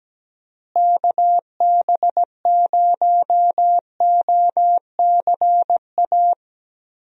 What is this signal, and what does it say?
Morse code, heard as a keyed tone, about 17 wpm, 700 Hz. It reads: KB0OCA
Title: USA Call Signs - 6 Characters - Repeated 2x - 17wpm